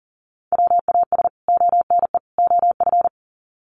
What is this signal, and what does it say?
Morse code, heard as a keyed tone, about 40 wpm, 700 Hz. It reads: WAS ONE OF
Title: Sets of 3 Words 40wpm